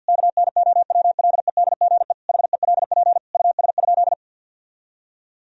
Morse code, heard as a keyed tone, about 51 wpm, 700 Hz. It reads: KNOWLEDGE HELP US?
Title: Sentences from Top 1000 Words - Review Part 3 - 50wpm